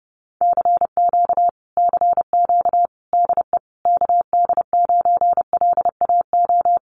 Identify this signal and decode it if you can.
Morse code, heard as a keyed tone, about 30 wpm, 700 Hz. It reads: CQ CQ DE KD9LAO
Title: Common QSO Phrases Encore 30wpm